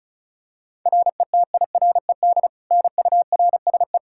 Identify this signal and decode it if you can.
Morse code, heard as a keyed tone, about 36 wpm, 700 Hz. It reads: RETIRED NURSE
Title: Practice QSO - Exchange 3 - More Info - 35wpm